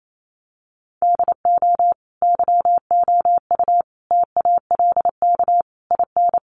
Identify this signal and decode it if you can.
Morse code, heard as a keyed tone, about 28 wpm, 700 Hz. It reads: DO YOU TALK IN
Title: Sentences from Top 500 Words 28wpm